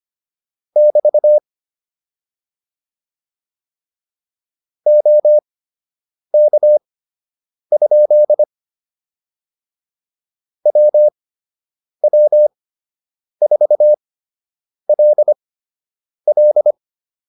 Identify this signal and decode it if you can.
Morse code, heard as a keyed tone, about 25 wpm, spaced as for 8 wpm, 600 Hz. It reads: <BT> OK? WW4LL